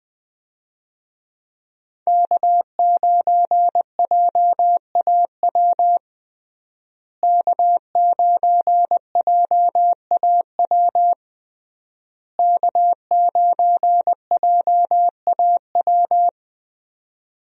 Morse code, heard as a keyed tone, about 20 wpm, 700 Hz. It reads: K9JAW K9JAW K9JAW
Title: USA Call Signs - 5 Characters - Repeated 3x - 20wpm